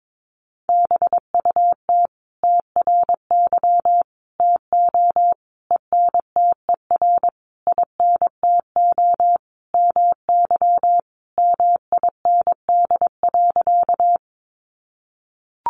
Morse code, heard as a keyed tone, about 22 wpm, 700 Hz. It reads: BUT TRY TO ENTER INTO MY MIND.
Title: Sentences from Top 1300 Words 22wpm